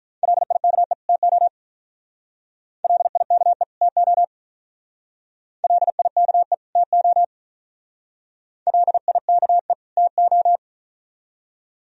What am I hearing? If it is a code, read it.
Morse code, heard as a keyed tone, about 44 wpm, 700 Hz. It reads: LIKE TO LIKE TO LIKE TO LIKE TO